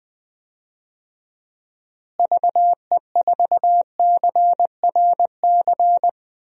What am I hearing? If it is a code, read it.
Morse code, heard as a keyed tone, about 20 wpm, 700 Hz. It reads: VE4CRC
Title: DXCC-100 Call Signs - 6 Characters - Repeated 2x - 20wpm